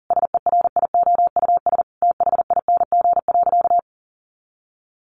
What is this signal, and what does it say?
Morse code, heard as a keyed tone, about 40 wpm, 700 Hz. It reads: SERIOUS THING.
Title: Sentences from Top 1500 Words 40wpm